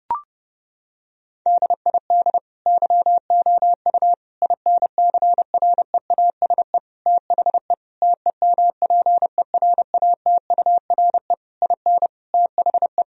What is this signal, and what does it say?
Morse code, heard as a keyed tone, about 32 wpm, 700 Hz. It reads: DID YOU INCREASE THE TEMPERATURE IN THE
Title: Sentences from Top 1000 Words - Review Part 4 - 30wpm